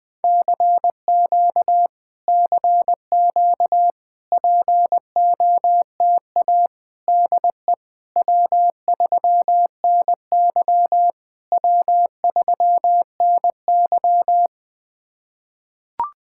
Morse code, heard as a keyed tone, about 21 wpm, 700 Hz. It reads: CQ CQ POTA DE W3NY W3NY
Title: POTA QSO Practice - 20wpm